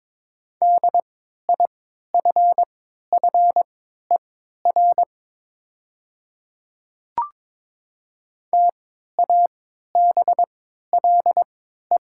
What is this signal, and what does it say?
Morse code, heard as a keyed tone, about 22 wpm, spaced as for 12 wpm, 700 Hz. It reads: DIFFER TABLE
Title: 3x Character Spacing - Top 500 Words - 22wpm